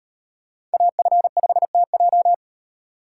Morse code, heard as a keyed tone, about 38 wpm, 700 Hz. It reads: AF5TJ